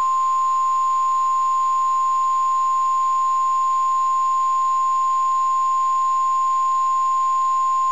Emergency Alert System (SAME) tones (the 1050 Hz attention tone). It is heard from 1.0 to 7.1 s.
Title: KEC94 NOAA Weather Radio